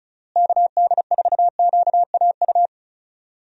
Morse code, heard as a keyed tone, about 34 wpm, 700 Hz. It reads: KD4QAU
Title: USA Call Signs - 6 Characters - Repeated 3x - 35wpm